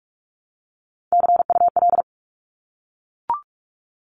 Morse code, heard as a keyed tone, about 44 wpm, 700 Hz. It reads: CUL
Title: Abbreviations Repeated 3x - 45wpm